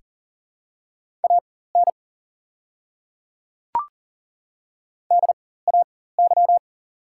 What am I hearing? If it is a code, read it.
Morse code, heard as a keyed tone, about 40 wpm, spaced as for 19 wpm, 690 Hz. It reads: AN DAY
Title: 4x Character Spacing - Top 100 Words - 40wpm